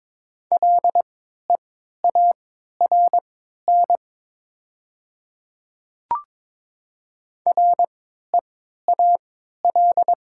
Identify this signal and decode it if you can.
Morse code, heard as a keyed tone, about 22 wpm, spaced as for 13 wpm, 700 Hz. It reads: LEARN REAL
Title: 3x Character Spacing - Top 300 Words - 22wpm